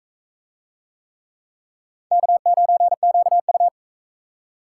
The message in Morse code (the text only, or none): K9QU